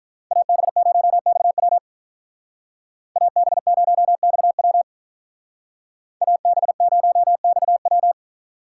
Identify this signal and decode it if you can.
Morse code, heard as a keyed tone, about 46 wpm, 700 Hz. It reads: AB0XW AB0XW AB0XW